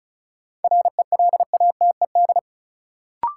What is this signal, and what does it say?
Morse code, heard as a keyed tone, about 36 wpm, 700 Hz. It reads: RELATED
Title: Sentences from Top 1000 Words - Review Part 5 - New Word Drill - 35wpm